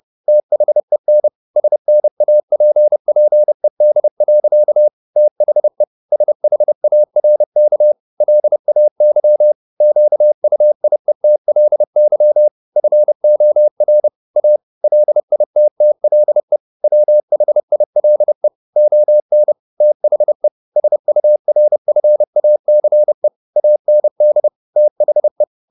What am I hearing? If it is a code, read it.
Morse code, heard as a keyed tone, about 30 wpm, 600 Hz. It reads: THEN SNAPPED. THE SHARK LAY QUIETLY FOR A LITTLE WHILE ON THE SURFACE AND THE